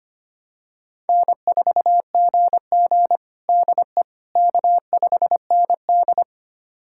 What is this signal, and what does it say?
Morse code, heard as a keyed tone, about 25 wpm, 700 Hz. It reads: N4GG DE K5ND
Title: USA Callsign DE Callsign - 4 Characters - 25wpm